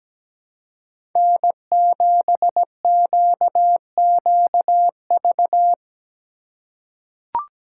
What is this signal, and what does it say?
Morse code, heard as a keyed tone, about 17 wpm, 700 Hz. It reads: N7QQV
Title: USA Call Signs - 4x Character Spacing - 17wpm